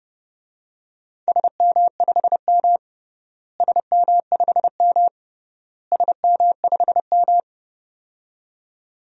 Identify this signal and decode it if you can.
Morse code, heard as a keyed tone, about 30 wpm, 700 Hz. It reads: SM5M SM5M SM5M
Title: DXCC-100 Call Signs - 4 Characters - Repeated 3x - 30wpm